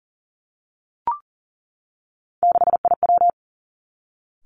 Morse code, heard as a keyed tone, about 42 wpm, 700 Hz. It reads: 6IW
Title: Three Challenging Letter-Number Combinations 40wpm